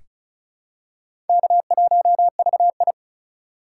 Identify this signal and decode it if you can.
Morse code, heard as a keyed tone, about 34 wpm, 700 Hz. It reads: K1VI
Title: USA Call Signs - 4 Characters - 6x Character Spacing - 35wpm